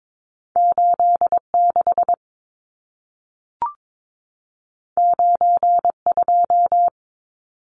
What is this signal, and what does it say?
Morse code, heard as a keyed tone, about 22 wpm, 700 Hz. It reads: 86 92
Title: Numbers - 2-Digit 22wpm